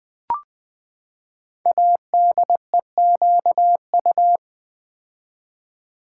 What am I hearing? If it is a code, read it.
Morse code, heard as a keyed tone, about 20 wpm, 700 Hz. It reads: ADEQU
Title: Build A Word - 5 Letters At A Time - 10-Letter Words - 20wpm